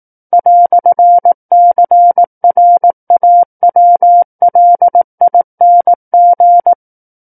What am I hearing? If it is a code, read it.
Morse code, heard as a keyed tone, about 18 wpm, 700 Hz. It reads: "CRAWLING
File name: 18 WPM